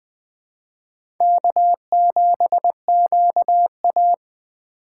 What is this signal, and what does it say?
Morse code, heard as a keyed tone, about 20 wpm, 700 Hz. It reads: K7QA
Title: USA Call Signs - 4 Characters - 4x Character Spacing - 20wpm